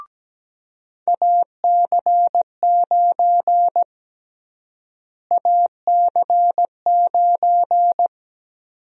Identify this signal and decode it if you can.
Morse code, heard as a keyed tone, about 17 wpm, 700 Hz. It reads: AC9 AC9
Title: USA Call Sign Prefixes - Repeated 2x - 17wpm